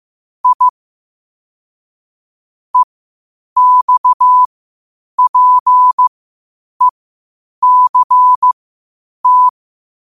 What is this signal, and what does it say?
Morse code, heard as a keyed tone, about 15 wpm, spaced as for 8 wpm, 1000 Hz. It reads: I EXPECT